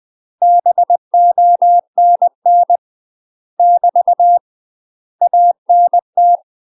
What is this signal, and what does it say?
Morse code, heard as a keyed tone, about 20 wpm, 700 Hz. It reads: BONN <BT> ANT
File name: UK MASTER QSOs TRAINING / 20WPM / UK MASTER QSOs 0063